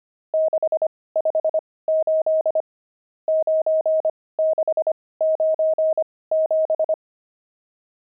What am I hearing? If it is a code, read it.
Morse code, heard as a keyed tone, about 25 wpm, spaced as for 18 wpm, 630 Hz. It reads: 658 9697
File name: pangram-fs-18-cs-25-nums4